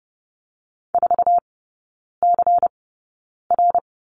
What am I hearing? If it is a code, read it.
Morse code, heard as a keyed tone, about 30 wpm, spaced as for 9 wpm, 690 Hz. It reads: 4CR
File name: Lesson 16e - Letter c - Three Characters 30wpm